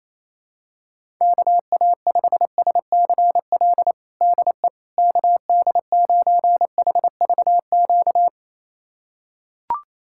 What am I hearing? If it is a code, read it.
Morse code, heard as a keyed tone, about 28 wpm, 700 Hz. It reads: KA5SCL DE KD9HVQ
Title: USA Callsign DE Callsign - 6 Characters - 28wpm